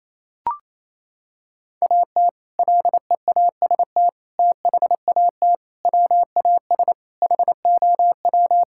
Morse code, heard as a keyed tone, about 28 wpm, 700 Hz. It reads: AT LEAST THAT WAS HOW
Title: Sentences from Top 1600 Words 28wpm